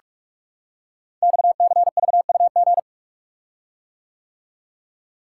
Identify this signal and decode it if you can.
Morse code, heard as a keyed tone, about 44 wpm, 700 Hz. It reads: XCVUG
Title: Code Groups - 5 Letter:Numbers - 45wpm